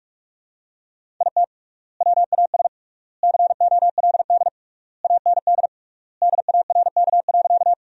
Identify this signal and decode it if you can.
Morse code, heard as a keyed tone, about 46 wpm, 700 Hz. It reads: IT WAS COLD AND DARK.
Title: 3x Word-Spacing - Sentences from Top 500 Words - 45wpm